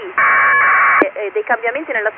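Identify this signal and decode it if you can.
Packet radio (AX.25, as in APRS), with speech.
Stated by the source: Recording of an ARISS school contact with the International Space Station through a ground station in Bari, Italy.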